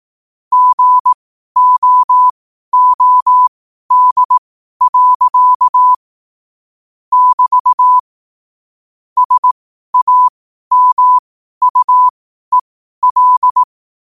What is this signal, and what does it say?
Morse code, heard as a keyed tone, about 18 wpm, spaced as for 12 wpm, 1000 Hz. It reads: GOOD. <BT> SAMUEL
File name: qotdfiles / 12wpm